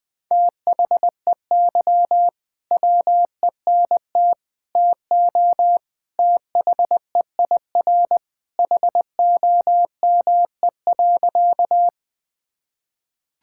Morse code, heard as a keyed tone, about 20 wpm, 700 Hz. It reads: THEY WENT TO THEIR HOME.
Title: Sentences with 2-Word Phrases - 20wpm